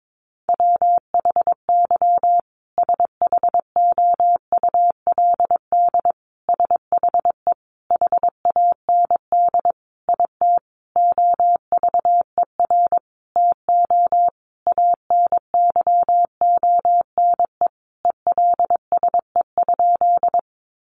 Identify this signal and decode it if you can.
Morse code, heard as a keyed tone, about 22 wpm, 700 Hz. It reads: WHY SHOULD SHE HAND IT OVER TO ANYONE ELSE?
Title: Sentences from Top 900 Words 22wpm